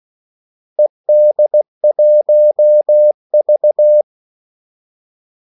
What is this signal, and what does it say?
Morse code, heard as a keyed tone, about 16 wpm, 600 Hz. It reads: ED1V
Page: Kaldesignaler 16 wpm